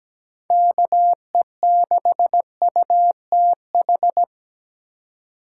Morse code, heard as a keyed tone, about 17 wpm, 700 Hz. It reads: KE6UTH
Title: USA Call Signs - 6 Characters - 3x Character Spacing - 17wpm